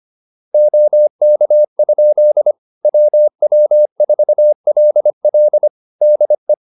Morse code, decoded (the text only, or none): OK? WW4LL DE